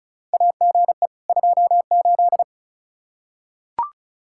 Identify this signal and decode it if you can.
Morse code, heard as a keyed tone, about 36 wpm, 700 Hz. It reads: AGE 28
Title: Practice QSO - Exchange 3 - More Info - 35wpm